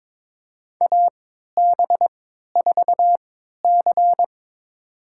Morse code, heard as a keyed tone, about 21 wpm, spaced as for 12 wpm, 700 Hz. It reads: AB4C
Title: USA Call Signs - 3x Character Spacing - 22wpm